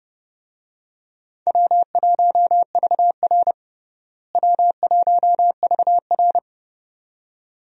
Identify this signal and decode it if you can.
Morse code, heard as a keyed tone, about 30 wpm, 700 Hz. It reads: W1VR W1VR
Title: USA Call Signs - 4 Characters - Repeated 2x - 30wpm